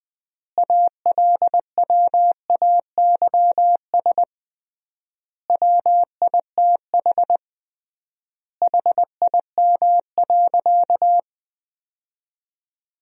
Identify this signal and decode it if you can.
Morse code, heard as a keyed tone, about 21 wpm, 700 Hz. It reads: ALWAYS WITH HIM.
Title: 3x Word-Spacing - Sentences from Top 400 Words - 20wpm